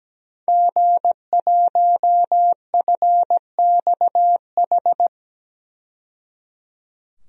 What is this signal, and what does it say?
Morse code, heard as a keyed tone, about 17 wpm, 700 Hz. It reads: G1FXH
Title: DXCC-100 Call Signs - 5 Characters - 17wpm